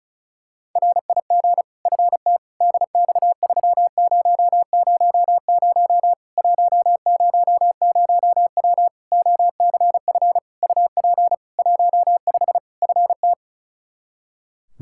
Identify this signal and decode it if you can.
Morse code, heard as a keyed tone, about 36 wpm, 700 Hz. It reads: RIG FT DX3000 100W OCF UP 15 FT
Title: K1USN SST QSO Practice 35wpm